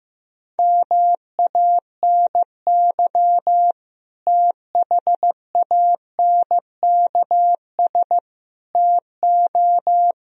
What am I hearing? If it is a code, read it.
Morse code, heard as a keyed tone, about 16 wpm, 700 Hz. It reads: MANY THANKS TO
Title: Sentences from Top 1000 Words - Review Part 3 - 15wpm